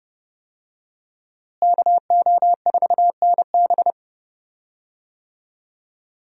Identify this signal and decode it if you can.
Morse code, heard as a keyed tone, about 31 wpm, 700 Hz. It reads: KO4NB
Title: USA Call Signs - 5 Characters - Repeated 3x - 30wpm